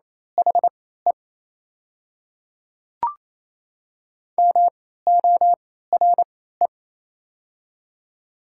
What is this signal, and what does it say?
Morse code, heard as a keyed tone, about 27 wpm, spaced as for 16 wpm, 700 Hz. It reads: HE MORE
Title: 3x Character Spacing - Top 100 Words - 28wpm